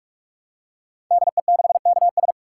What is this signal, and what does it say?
Morse code, heard as a keyed tone, about 44 wpm, 700 Hz. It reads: DE6KS
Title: Code Groups - 5 Letter:Numbers - 45wpm